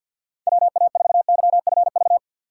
Morse code, heard as a keyed tone, about 53 wpm, 690 Hz. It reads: WA4YFV
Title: USA Call Signs - 6 Characters - 8x Character Spacing - 50wpm